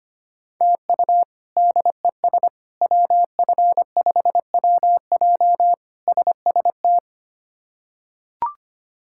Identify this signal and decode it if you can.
Morse code, heard as a keyed tone, about 25 wpm, 700 Hz. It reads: TU DES WF5WJ SST
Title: K1USN SST QSO Practice - 25wpm